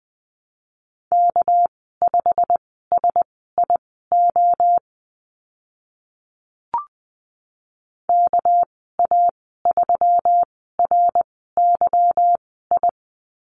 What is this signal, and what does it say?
Morse code, heard as a keyed tone, about 20 wpm, spaced as for 14 wpm, 700 Hz. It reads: K5SIO KA3RYI
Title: 213 Lesson 34g - Number 8 - Call Signs 20wpm